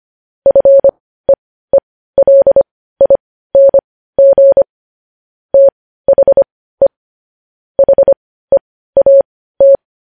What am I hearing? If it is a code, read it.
Morse code, heard as a keyed tone, about 25 wpm, spaced as for 15 wpm, 550 Hz. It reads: FEELING THE HEAT
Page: news headlines in morse code at 15 wpm